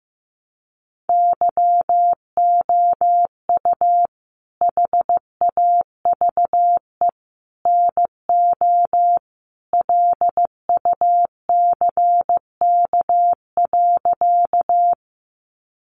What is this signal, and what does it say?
Morse code, heard as a keyed tone, about 15 wpm, 700 Hz. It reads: YOU HAVE NO LUCK.
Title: Sentences from Top 1500 Words 15wpm